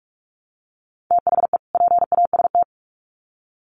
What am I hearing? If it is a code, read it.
Morse code, heard as a keyed tone, about 44 wpm, 700 Hz. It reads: THE PAST
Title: Sets of 2 Words 45wpm